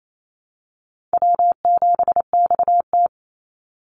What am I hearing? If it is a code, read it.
Morse code, heard as a keyed tone, about 28 wpm, 700 Hz. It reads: W7XT
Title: Easy Call Signs Repeated 2x - 28wpm